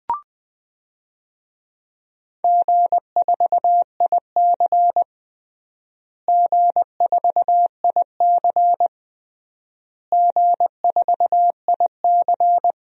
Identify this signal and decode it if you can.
Morse code, heard as a keyed tone, about 20 wpm, 690 Hz. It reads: G4IC G4IC G4IC
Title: DXCC-100 Call Signs - 4 Characters - Repeated 3x - 20wpm